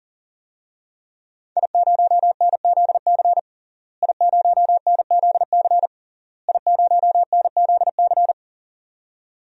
Morse code, heard as a keyed tone, about 40 wpm, 700 Hz. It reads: I0NZC I0NZC I0NZC
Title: DXCC-100 Call Signs - 5 Characters - Repeated 3x - 40wpm